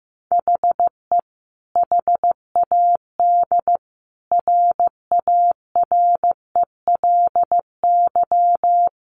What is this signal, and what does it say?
Morse code, heard as a keyed tone, about 15 wpm, 700 Hz. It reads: HE HAD RARELY
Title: Sentences from Top 1400 Words 15wpm